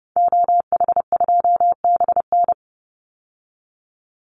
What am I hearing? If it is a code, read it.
Morse code, heard as a keyed tone, about 30 wpm, 700 Hz. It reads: OH2BN
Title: Call Signs Repeated 2x - 30wpm